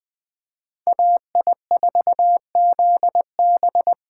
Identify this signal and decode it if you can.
Morse code, heard as a keyed tone, about 24 wpm, 700 Hz. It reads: AI4ZB
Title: Speed Racing - USA 2x2 Call Signs 20wpm